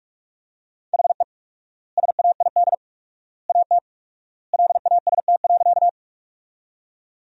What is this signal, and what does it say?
Morse code, heard as a keyed tone, about 47 wpm, 700 Hz. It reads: HE SAID AT LAST.
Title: 4x Word-Spacing - Sentences from Top 300 Words - 45wpm